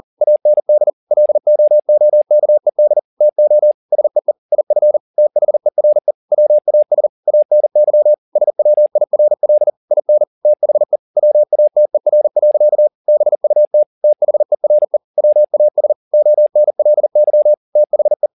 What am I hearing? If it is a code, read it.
Morse code, heard as a keyed tone, about 40 wpm, 600 Hz. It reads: AND LOOKED TO SEE IF THERE WAS ANY SWIRL IN THE WATER. BUT THERE WAS ONLY THE
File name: the old man and the sea 2 40wpm